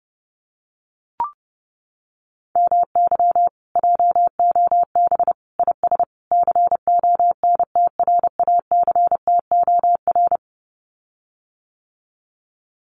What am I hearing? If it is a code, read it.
Morse code, heard as a keyed tone, about 30 wpm, 700 Hz. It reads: MY JOB IS CONTRACTOR
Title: Common QSO Phrases Encore 30wpm